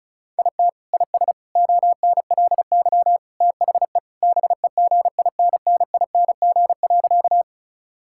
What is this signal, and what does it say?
Morse code, heard as a keyed tone, about 34 wpm, 700 Hz. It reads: IT IS ONLY THE BEGINNING.
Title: Sentences from Top 1000 Words - Review Part 5 - 35wpm